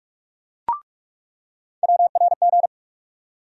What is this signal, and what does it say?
Morse code, heard as a keyed tone, about 41 wpm, 700 Hz. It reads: WRG
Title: USA Call Sign Postfixes - 3 Letters 45wpm